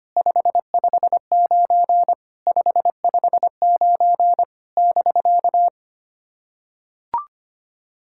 Morse code, heard as a keyed tone, about 25 wpm, 700 Hz. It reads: 559 559 <BK>
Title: SOTA QSO Practice 25wpm